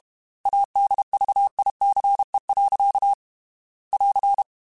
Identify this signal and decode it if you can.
Morse code, heard as a keyed tone, about 32 wpm, 800 Hz. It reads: ADVICE. <AR>